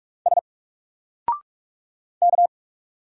Morse code, heard as a keyed tone, about 44 wpm, 700 Hz. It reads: S K
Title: Single Letter-Number Alt 45wpm